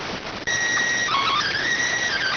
SSTV (slow-scan television), PD 290.